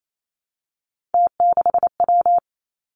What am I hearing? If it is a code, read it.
Morse code, heard as a keyed tone, about 28 wpm, 700 Hz. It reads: T6W
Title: Three Challenging Letter-Number Combinations 28wpm